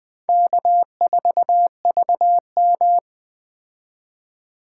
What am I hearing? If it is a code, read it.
Morse code, heard as a keyed tone, about 21 wpm, 700 Hz. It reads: K4VM